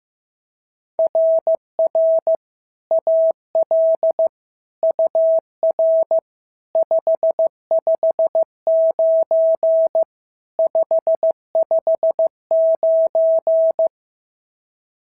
Morse code, heard as a keyed tone, about 13 wpm, 650 Hz. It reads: RR AL UR 559 559
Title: SOTA QSO Practice 15wpm